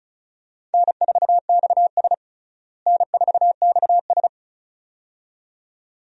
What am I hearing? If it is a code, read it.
Morse code, heard as a keyed tone, about 36 wpm, 700 Hz. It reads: N4XS N4XS
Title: USA Call Signs - 4 Characters - Repeated 2x - 35wpm